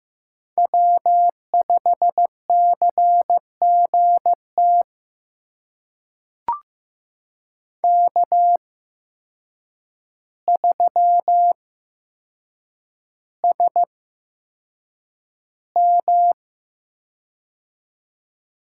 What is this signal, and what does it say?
Morse code, heard as a keyed tone, about 15 wpm, spaced as for 7 wpm, 700 Hz. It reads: W5CGT K3SM
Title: USA Call Signs - 8x Character Spacing - 15wpm